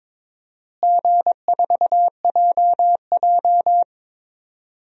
Morse code, heard as a keyed tone, about 22 wpm, 700 Hz. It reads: G4JJ
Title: DXCC-100 Call Signs - 4 Characters - Repeated 3x - 22wpm